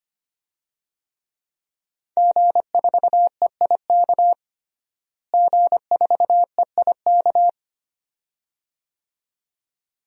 Morse code, heard as a keyed tone, about 25 wpm, 700 Hz. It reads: G4EIK G4EIK
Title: DXCC-100 Call Signs - 5 Characters - Repeated 2x - 25wpm